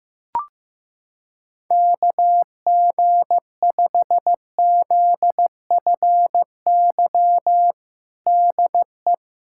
Morse code, heard as a keyed tone, about 15 wpm, 700 Hz. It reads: KG5ZFY DE
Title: USA Callsign DE Callsign - 6 Characters - 15wpm